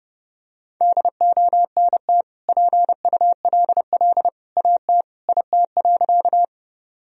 Morse code, heard as a keyed tone, about 30 wpm, 700 Hz. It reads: DONT PULL AT IT.
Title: Sentences from Top 1000 Words - Review Part 2 - 30wpm